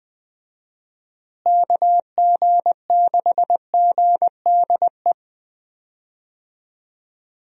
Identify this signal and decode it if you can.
Morse code, heard as a keyed tone, about 20 wpm, 700 Hz. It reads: KG6GDE
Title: USA Call Signs - 6 Characters - Repeated 2x - 20wpm